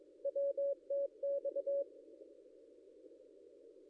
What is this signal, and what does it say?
Morse code, heard as a keyed tone, about 23 wpm, 560 Hz. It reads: WTX
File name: contests / ss-cw-k5tr-op-n5rz -0800